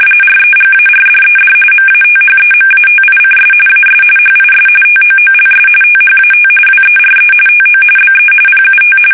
RTTY, about 45.45 baud. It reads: REGARDED MATHEMATICS AS HIS TOOLS THAT WERE TO BE PUT
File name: fsk850 / benjamin-g-lamme